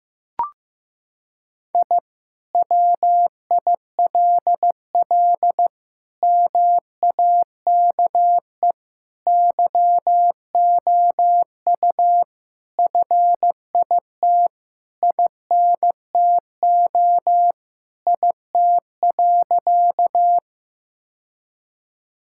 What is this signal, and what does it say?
Morse code, heard as a keyed tone, about 16 wpm, 700 Hz. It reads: I WILL MAKE YOU FIT INTO IT.
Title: Sentences from Top 1000 Words - Review Part 5 - 15wpm